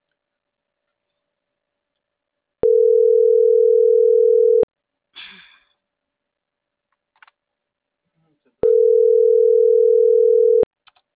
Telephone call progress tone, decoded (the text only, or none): ringback tone